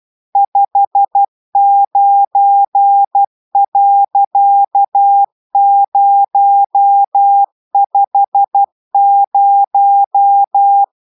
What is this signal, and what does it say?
Morse code, heard as a keyed tone, about 12 wpm, 800 Hz. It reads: 59.050